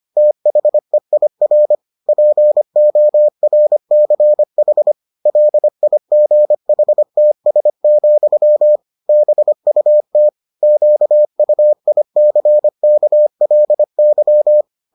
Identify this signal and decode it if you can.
Morse code, heard as a keyed tone, about 25 wpm, 600 Hz. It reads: THEIR PORCH LIGHTS, BUT QUICKLY